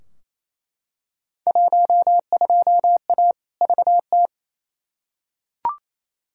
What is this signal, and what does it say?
Morse code, heard as a keyed tone, about 28 wpm, 700 Hz. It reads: 12A VT
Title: Field Day Exchanges Encore - 28wpm